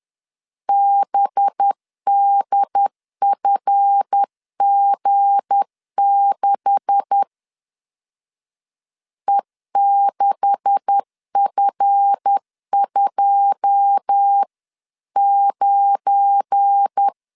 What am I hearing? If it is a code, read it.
Morse code, heard as a keyed tone, about 10 wpm, 800 Hz. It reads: BDFG6 E6F29
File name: CW / DET / 1 9 1500C S A 10wpm 4511